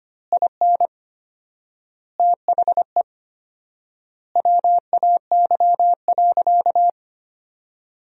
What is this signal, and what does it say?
Morse code, heard as a keyed tone, about 25 wpm, 700 Hz. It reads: IN THE WAY.
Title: 4x Word-Spacing - Sentences from Top 200 Words - 25wpm